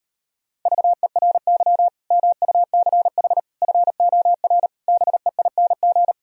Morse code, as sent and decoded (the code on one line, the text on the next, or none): ...- . .-. -.-- / -- ..- -.-. .... / ..-. --- .-. / -... .. -. --.
VERY MUCH FOR BEING